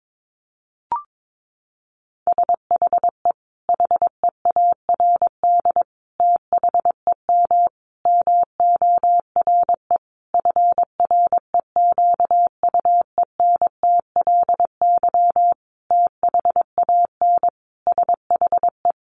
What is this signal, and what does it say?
Morse code, heard as a keyed tone, about 22 wpm, 700 Hz. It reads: SHE HEARD THEM MORE FREQUENTLY THAN SHE
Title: Sentences from Top 1300 Words 22wpm